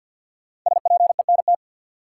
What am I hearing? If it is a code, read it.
Morse code, heard as a keyed tone, about 46 wpm, 700 Hz. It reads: SPENT
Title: Sets of 5 Words - New Word Drill - 50wpm